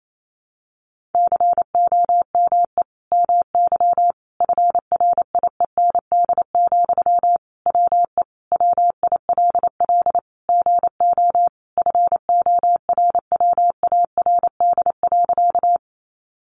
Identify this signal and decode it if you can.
Morse code, heard as a keyed tone, about 28 wpm, 700 Hz. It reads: COME MY FRIEND, WE WILL GO FORWARD.
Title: Sentences from Top 1300 Words 28wpm